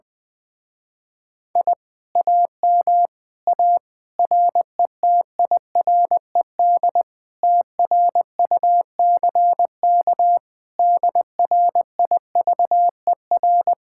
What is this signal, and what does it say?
Morse code, heard as a keyed tone, about 20 wpm, 700 Hz. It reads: I AM A RETIRED TRUCK DRIVER
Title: Speed-Racing Common QSO Phrases - 20wpm